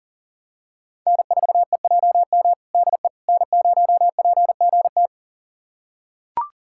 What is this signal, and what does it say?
Morse code, heard as a keyed tone, about 40 wpm, 700 Hz. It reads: N4EJM DE N0PGT
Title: USA Callsign DE Callsign - 5 Characters - 40wpm